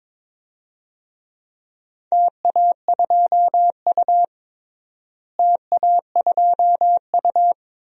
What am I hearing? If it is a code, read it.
Morse code, heard as a keyed tone, about 22 wpm, 700 Hz. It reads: TA2U TA2U
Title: DXCC-100 Call Signs - 4 Characters - Repeated 2x - 22wpm